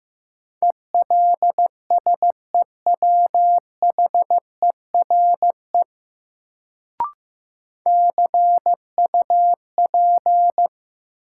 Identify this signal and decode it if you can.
Morse code, heard as a keyed tone, about 15 wpm, 700 Hz. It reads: ELSEWHERE CUP
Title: Sentences from Top 1600 Words - New Word Drill - 15wpm